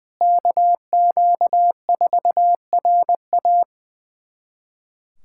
Morse code, heard as a keyed tone, about 20 wpm, 700 Hz. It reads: KQ4RA
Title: USA Callsign DE Callsign - 5 Characters - 20wpm